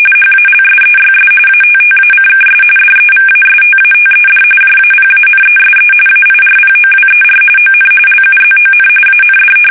RTTY, about 45.45 baud. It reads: HALSEYS ATTACKS DEVASTATED JAPANESE AIRPOWER ON LUZON AND
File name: fsk850 / battleshipnj